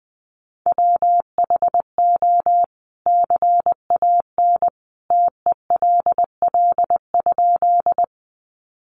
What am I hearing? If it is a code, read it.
Morse code, heard as a keyed tone, about 19 wpm, 700 Hz. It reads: WHO CAN TELL?
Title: A Sound Of Thunder 20wpm